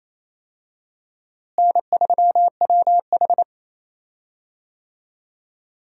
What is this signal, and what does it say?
Morse code, heard as a keyed tone, about 28 wpm, 700 Hz. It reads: N3WH
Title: USA Call Signs - 4 Characters - Repeated 3x - 28wpm